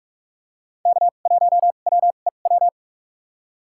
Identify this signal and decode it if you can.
Morse code, heard as a keyed tone, about 46 wpm, spaced as for 33 wpm, 700 Hz. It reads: K1WEW
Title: USA Call Signs - 5 Characters - 2x Character Spacing - 45wpm